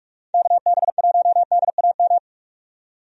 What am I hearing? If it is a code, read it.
Morse code, heard as a keyed tone, about 46 wpm, 700 Hz. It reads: KB1DAM
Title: USA Call Signs - 6 Characters - 45wpm